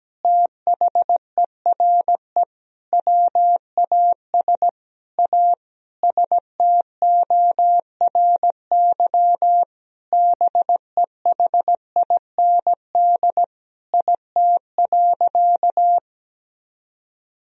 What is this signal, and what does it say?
Morse code, heard as a keyed tone, about 17 wpm, 700 Hz. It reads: THERE WAS A STORY BEHIND IT.
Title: Sentences from Top 500 Words and Abbreviations 17wpm